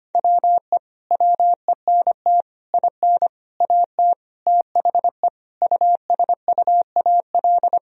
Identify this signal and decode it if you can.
Morse code, heard as a keyed tone, about 25 wpm, 700 Hz. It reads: WE WENT IN AT THE USUAL
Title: Sentences from Top 1000 Words - Review Part 2 - 25wpm